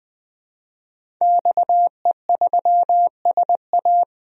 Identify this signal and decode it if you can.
Morse code, heard as a keyed tone, about 20 wpm, 700 Hz. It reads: XE3SA
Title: DXCC-100 Call Signs - 5 Characters - Repeated 2x - 20wpm